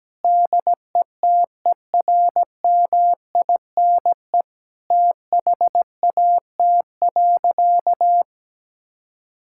Morse code, heard as a keyed tone, about 17 wpm, 700 Hz. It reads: DETERMINE THAT.
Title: Sentences from Top 1000 Words - Review Part 5 - 17wpm